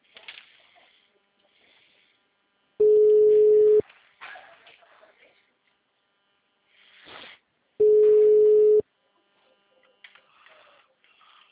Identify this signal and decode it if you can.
Ringback tone (Europe, 425 Hz).